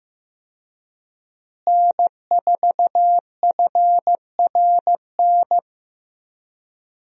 Morse code, heard as a keyed tone, about 15 wpm, 700 Hz. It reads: N4FRN
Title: USA Call Signs - 5 Characters - Repeated 3x - 15wpm